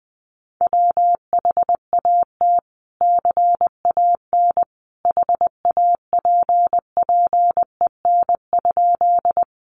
Morse code, heard as a keyed tone, about 20 wpm, 700 Hz. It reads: WHAT CAN HAPPEN?
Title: Sentences from Top 400 Words 20wpm